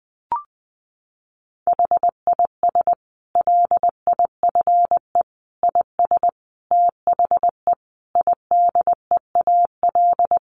Morse code, heard as a keyed tone, about 19 wpm, 700 Hz. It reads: HIS LIFE IS THE IDEAL
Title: Sentences from Top 1400 Words 20wpm